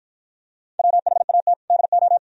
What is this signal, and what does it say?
Morse code, heard as a keyed tone, about 52 wpm, 700 Hz. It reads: WHAT DO